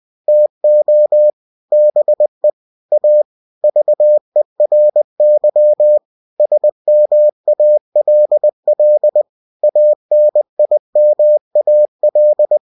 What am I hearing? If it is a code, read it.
Morse code, heard as a keyed tone, about 20 wpm, 600 Hz. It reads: TO BE A VERY SMALL ANIMAL
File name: cw-books / winnie-the-pooh / winnie-20wpm-page-0091